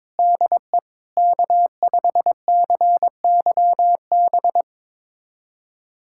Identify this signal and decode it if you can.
Morse code, heard as a keyed tone, about 21 wpm, 700 Hz. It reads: DE K5CYB
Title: USA Callsign DE Callsign - 5 Characters - 22wpm